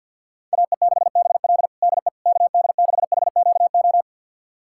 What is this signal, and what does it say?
Morse code, heard as a keyed tone, about 52 wpm, 700 Hz. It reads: AE6BL DE KD6HQO
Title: USA Callsign DE Callsign 50wpm